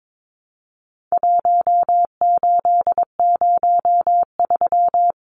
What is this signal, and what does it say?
Morse code, heard as a keyed tone, about 22 wpm, 700 Hz. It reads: 1803
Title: Numbers - 4-Digit 22wpm